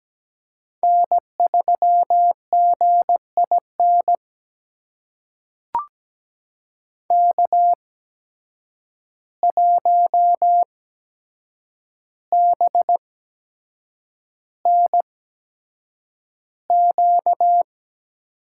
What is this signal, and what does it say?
Morse code, heard as a keyed tone, about 17 wpm, spaced as for 7 wpm, 700 Hz. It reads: N3GIN K1BNQ
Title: USA Call Signs - 5 Characters - 8x Character Spacing - 17wpm